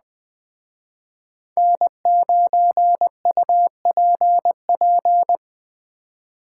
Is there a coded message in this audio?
Yes: Morse code from 1.6 to 5.4 s, heard as a keyed tone, about 20 wpm, 700 Hz. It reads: N9UPP